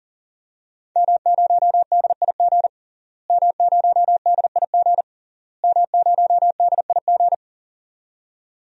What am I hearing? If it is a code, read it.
Morse code, heard as a keyed tone, about 40 wpm, 700 Hz. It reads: M0DIG M0DIG M0DIG
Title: DXCC-100 Call Signs - 5 Characters - Repeated 3x - 40wpm